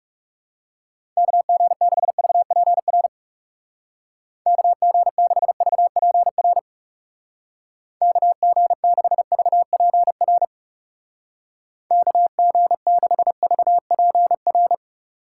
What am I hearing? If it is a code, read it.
Morse code, heard as a keyed tone, about 39 wpm, 700 Hz. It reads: KG6VPR KG6VPR KG6VPR KG6VPR